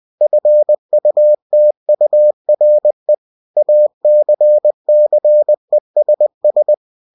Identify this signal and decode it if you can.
Morse code, heard as a keyed tone, about 20 wpm, 600 Hz. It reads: FUTURE ACCESS
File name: cw-books / winnie-the-pooh / winnie-20wpm-page-0113